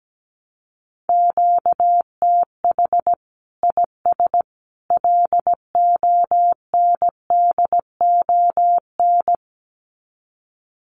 Morse code, heard as a keyed tone, about 17 wpm, 700 Hz. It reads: QTH IS LONDON
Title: Common QSO Phrases 17wpm